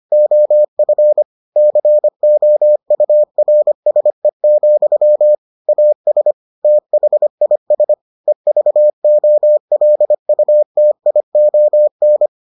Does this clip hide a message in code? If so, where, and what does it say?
Yes: Morse code from 0.1 to 12.3 s, heard as a keyed tone, about 25 wpm, 600 Hz. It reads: OF COURSE, AS THIS EVOLUTION